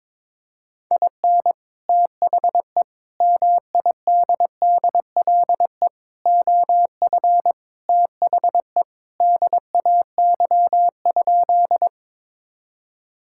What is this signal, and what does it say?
Morse code, heard as a keyed tone, about 22 wpm, 700 Hz. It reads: IN THE MIDDLE OF THE DAY?
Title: Sentences from Top 1000 Words - Review Part 6 - 22wpm